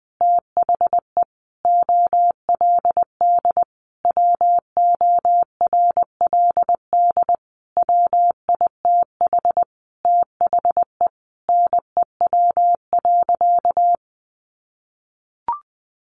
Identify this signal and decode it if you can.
Morse code, heard as a keyed tone, about 20 wpm, 700 Hz. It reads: THE OLD WORLD WITH THE NEW.
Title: Sentences from Top 1300 Words 20wpm